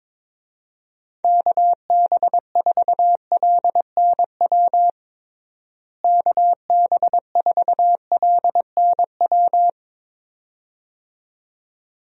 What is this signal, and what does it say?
Morse code, heard as a keyed tone, about 22 wpm, 700 Hz. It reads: KB4LNW KB4LNW
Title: USA Call Signs - 6 Characters - Repeated 2x - 22wpm